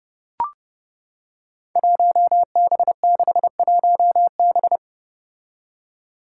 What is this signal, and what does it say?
Morse code, heard as a keyed tone, about 30 wpm, 700 Hz. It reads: 1B61B
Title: Five Challenging Letter-Number Combinations 30wpm